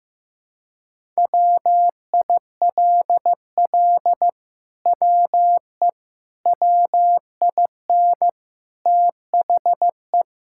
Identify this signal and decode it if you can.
Morse code, heard as a keyed tone, about 15 wpm, 700 Hz. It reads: WILL WE WIN THE
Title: Sentences from Top 1600 Words 15wpm